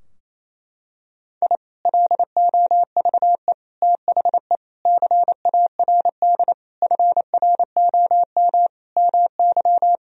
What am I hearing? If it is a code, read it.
Morse code, heard as a keyed tone, about 28 wpm, 700 Hz. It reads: I LOVE THE CARD FROM MY
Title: Sentences from Top 1000 Words - Review Part 4 - 28wpm